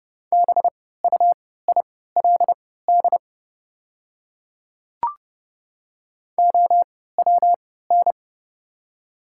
Morse code, heard as a keyed tone, about 32 wpm, spaced as for 17 wpm, 700 Hz. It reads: BUILD OWN
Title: 3x Character Spacing - Top 300 Words - 30wpm